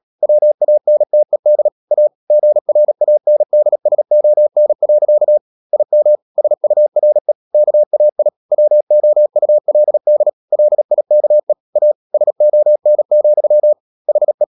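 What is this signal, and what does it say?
Morse code, heard as a keyed tone, about 37 wpm, 600 Hz. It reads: WANTED A GRANDSON. IM SURE KAI WOULD LIKE A SON, HE
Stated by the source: Morse code practice at 35–39 wpm sending English text of Ken Follett's novel Never, from CW med Gnister.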